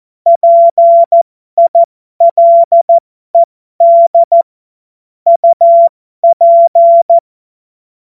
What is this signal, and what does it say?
Morse code, heard as a keyed tone, about 14 wpm, spaced as for 12 wpm, 680 Hz. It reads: PILED UP
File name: Morsekurs / Witze